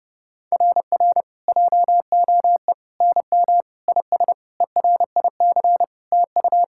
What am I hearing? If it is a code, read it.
Morse code, heard as a keyed tone, about 30 wpm, 700 Hz. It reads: RR JOE NM IS ERIC TU
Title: SOTA QSO Practice 30wpm